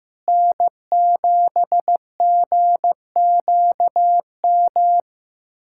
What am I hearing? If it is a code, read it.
Morse code, heard as a keyed tone, about 15 wpm, 700 Hz. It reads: N7GQM
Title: USA Callsign DE Callsign 15wpm